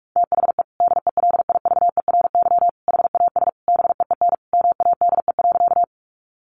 Morse code, heard as a keyed tone, about 44 wpm, 700 Hz. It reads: THE DELIVERY HAS BEEN MADE.
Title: Sentences from Top 1500 Words 45wpm